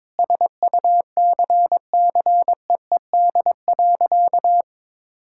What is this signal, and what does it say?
Morse code, heard as a keyed tone, about 22 wpm, 700 Hz. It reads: SUCCEED.
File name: Sentences with 9-Word Phrases - 22wpm